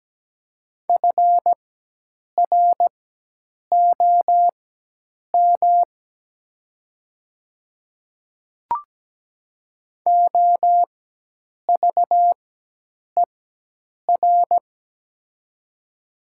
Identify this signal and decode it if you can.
Morse code, heard as a keyed tone, about 17 wpm, spaced as for 8 wpm, 700 Hz. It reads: FROM OVER
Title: 4x Character Spacing - Top 100 Words - 17wpm